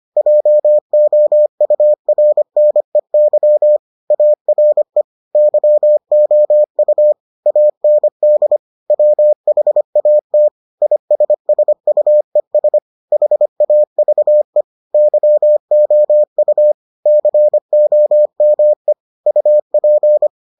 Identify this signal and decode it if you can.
Morse code, heard as a keyed tone, about 25 wpm, 600 Hz. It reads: JOURNEY ARE YOU AND WHAT ISSUES HAVE YOU COME UP